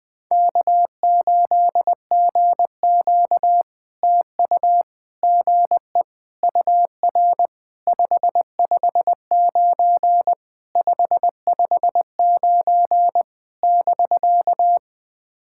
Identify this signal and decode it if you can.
Morse code, heard as a keyed tone, about 20 wpm, 700 Hz. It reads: K8GQ TU GE UR 559 559 <BK>
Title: POTA QSO Practice - 20wpm